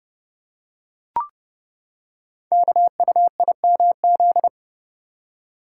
Morse code, heard as a keyed tone, about 28 wpm, 700 Hz. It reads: KUIMZ
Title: Code Groups - 5 Letters - 30wpm